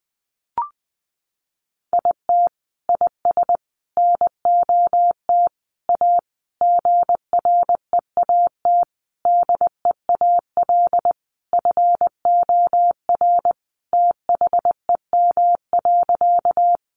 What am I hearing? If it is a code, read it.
Morse code, heard as a keyed tone, about 20 wpm, 700 Hz. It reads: IT IS NOT A GREAT DEAL FOR THEM.
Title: Sentences from Top 800 Words 20wpm